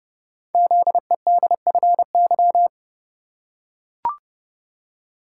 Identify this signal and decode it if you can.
Morse code, heard as a keyed tone, about 30 wpm, 700 Hz. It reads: ZEDFY
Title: Code Groups - 5 Letters - 30wpm